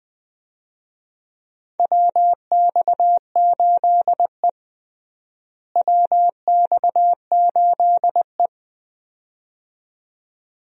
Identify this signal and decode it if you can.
Morse code, heard as a keyed tone, about 20 wpm, 700 Hz. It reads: WX8E WX8E
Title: USA Call Signs - 4 Characters - Repeated 2x - 20wpm